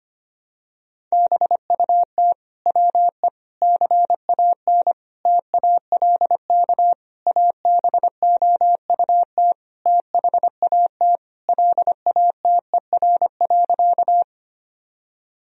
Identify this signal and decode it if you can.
Morse code, heard as a keyed tone, about 25 wpm, 700 Hz. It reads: BUT WE CAN TALK ABOUT THAT LATER.
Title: Sentences from Top 1000 Words - Review Part 4 - 25wpm